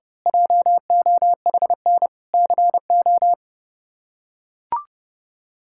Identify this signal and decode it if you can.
Morse code, heard as a keyed tone, about 30 wpm, 700 Hz. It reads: JOHN CO
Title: K1USN SST QSO Practice - 30wpm